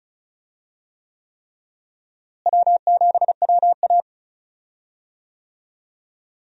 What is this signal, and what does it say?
Morse code, heard as a keyed tone, about 34 wpm, 700 Hz. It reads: W7WA
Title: USA Call Signs - 4 Characters - Repeated 3x - 35wpm